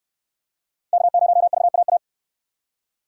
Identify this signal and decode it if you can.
Morse code, heard as a keyed tone, about 69 wpm, 700 Hz. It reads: K0VAA